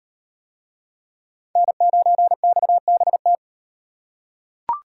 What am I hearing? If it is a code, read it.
Morse code, heard as a keyed tone, about 34 wpm, 700 Hz. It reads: N9XBT